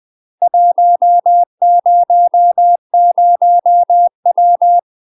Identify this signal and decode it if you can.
Morse code, heard as a keyed tone, about 20 wpm, 700 Hz. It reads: 100W
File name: UK MASTER QSOs TRAINING / 20WPM / UK MASTER QSOs 0035